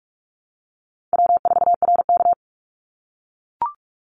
Morse code, heard as a keyed tone, about 44 wpm, 700 Hz. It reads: W4RK
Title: NA SOTA Chaser Callsigns 45wpm